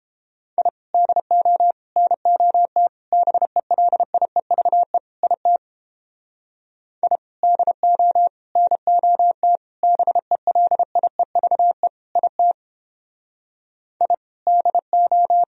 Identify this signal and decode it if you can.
Morse code, heard as a keyed tone, about 31 wpm, 700 Hz. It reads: I DO NOT BELIEVE IT I DO NOT BELIEVE IT I DO